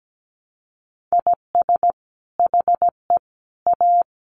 Morse code, heard as a keyed tone, about 17 wpm, 700 Hz. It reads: IS HE A
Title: Sentences from Top 300 Words 17wpm